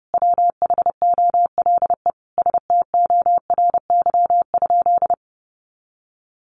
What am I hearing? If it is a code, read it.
Morse code, heard as a keyed tone, about 30 wpm, 700 Hz. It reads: WHOLE STORY?
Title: Sentences from Top 400 Words 30wpm